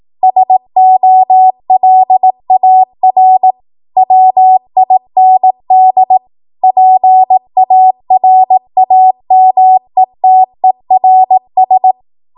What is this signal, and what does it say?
Morse code, heard as a keyed tone, about 18 wpm, 750 Hz. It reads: SOLAR WIND PARAMETERS
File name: Morse / 18WPM